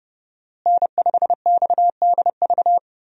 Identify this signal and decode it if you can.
Morse code, heard as a keyed tone, about 30 wpm, 700 Hz. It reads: N5XDV
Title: USA Call Signs - 5 Characters - Repeated 2x - 30wpm